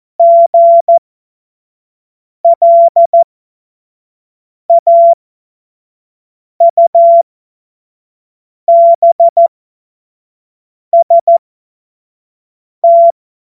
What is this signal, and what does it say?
Morse code, heard as a keyed tone, about 14 wpm, spaced as for 5 wpm, 680 Hz. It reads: GLAUBST
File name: Morsekurs / Witze / Witz 13. Dezember 2022 25 F70